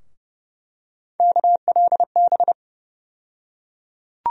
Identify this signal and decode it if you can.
Morse code, heard as a keyed tone, about 30 wpm, 700 Hz. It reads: KLB
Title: USA Call Sign Postfixes - 3 Letters 30wpm